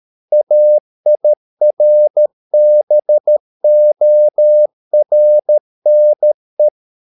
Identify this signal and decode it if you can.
Morse code, heard as a keyed tone, about 13 wpm, 600 Hz. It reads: AIRBORNE